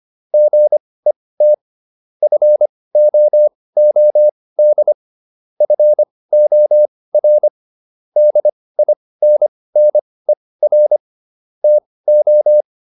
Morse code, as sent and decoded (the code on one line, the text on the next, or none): --. - / ..-. --- --- -.. / ..-. --- .-. / -.. .. -. -. .-. / - ---
GET FOOD FOR DINNER TO